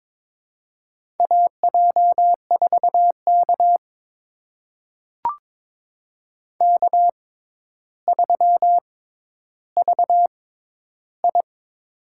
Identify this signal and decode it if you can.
Morse code, heard as a keyed tone, about 23 wpm, spaced as for 11 wpm, 700 Hz. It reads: AJ4K K3VI
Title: USA Call Signs - 4 Characters - 6x Character Spacing - 22wpm